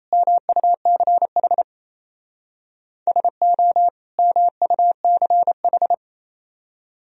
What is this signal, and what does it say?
Morse code, heard as a keyed tone, about 34 wpm, 700 Hz. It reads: MUCH SO MUCH
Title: Speed-Racing - Sets of 2 Words 28wpm